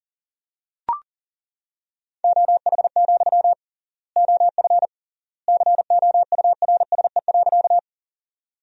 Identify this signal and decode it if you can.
Morse code, heard as a keyed tone, about 37 wpm, 700 Hz. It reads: OH, OF COURSE.
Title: 3x Word-Spacing - Sentences from Top 500 Words - 40wpm